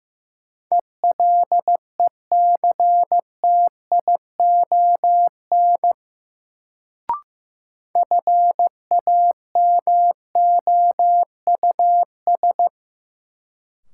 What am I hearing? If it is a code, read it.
Morse code, heard as a keyed tone, about 15 wpm, 700 Hz. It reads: ELECTION FAMOUS
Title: Sentences from Top 1500 Words - New Word Drill - 15wpm